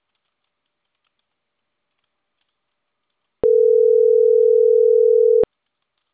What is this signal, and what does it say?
Ringback tone (North America), 440 + 480 Hz.
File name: CENTRAL CAEA / out agcaea01 - .26545